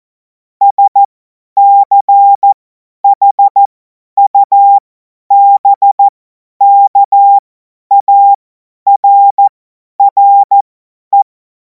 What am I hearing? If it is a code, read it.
Morse code, heard as a keyed tone, about 14 wpm, spaced as for 10 wpm, 800 Hz. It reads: SCHUBKARRE
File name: Morsekurs / Witze / Witz 23. Juni 2026 50 F70